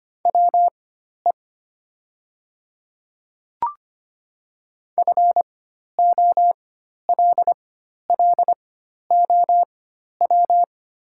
Morse code, heard as a keyed tone, about 25 wpm, spaced as for 12 wpm, 700 Hz. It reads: WE FOLLOW
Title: 4x Character Spacing - Top 200 Words - 25wpm